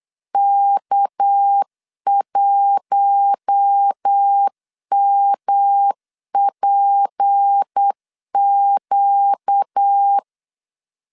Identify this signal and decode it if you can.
Morse code, heard as a keyed tone, about 8 wpm, 800 Hz. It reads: K1MPQ
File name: LEZIONI / 6 PKX0 / 6 PK 8wpm 3003